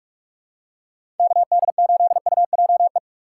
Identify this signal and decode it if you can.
Morse code, heard as a keyed tone, about 44 wpm, 700 Hz. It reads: KD8UJE
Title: USA Call Signs - 3x Character Spacing - 45wpm